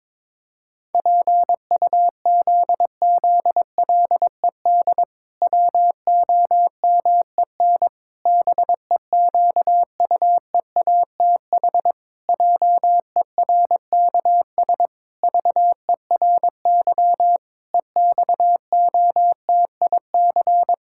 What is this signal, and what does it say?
Morse code, heard as a keyed tone, about 22 wpm, 700 Hz. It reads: PUZZLED WOMEN BEQUEATH JERKS VERY EXOTIC
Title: Pangram Challenge v2 - 22wpm